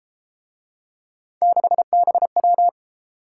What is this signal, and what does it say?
Morse code, heard as a keyed tone, about 31 wpm, 700 Hz. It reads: 6BW